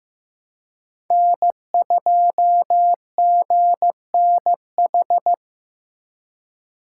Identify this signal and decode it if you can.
Morse code, heard as a keyed tone, about 18 wpm, 700 Hz. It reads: N2GNH
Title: Speed Racing - USA 1x3 Call Signs 15wpm